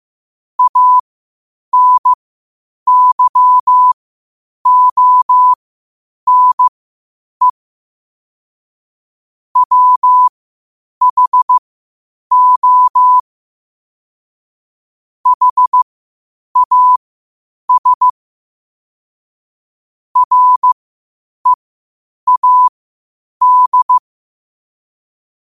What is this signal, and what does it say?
Morse code, heard as a keyed tone, about 15 wpm, spaced as for 8 wpm, 1000 Hz. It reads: ANYONE WHO HAS READ
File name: qotdfiles / 8wpm